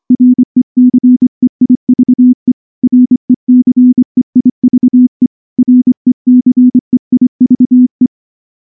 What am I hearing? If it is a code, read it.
Morse code, heard as a keyed tone, about 24 wpm, 250 Hz. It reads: RECEIVE RECEIVE RECEIVE
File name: morse / morse-training-25WPM